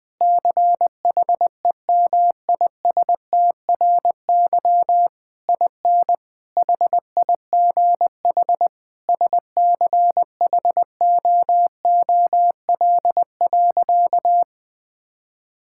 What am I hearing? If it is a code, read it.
Morse code, heard as a keyed tone, about 20 wpm, 700 Hz. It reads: CHEMISTRY IN HIGH SCHOOL.
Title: Sentences from Top 1000 Words - Review Part 6 - 20wpm